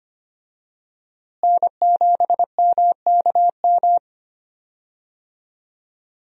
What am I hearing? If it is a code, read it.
Morse code, heard as a keyed tone, about 25 wpm, 700 Hz. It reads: N7MKM
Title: USA Call Signs - 5 Characters - Repeated 2x - 25wpm